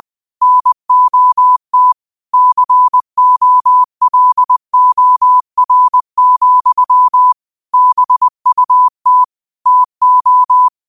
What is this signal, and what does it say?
Morse code, heard as a keyed tone, about 20 wpm, 1000 Hz. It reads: NOT COLOR, BUT TO
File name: qotdfiles / 20wpm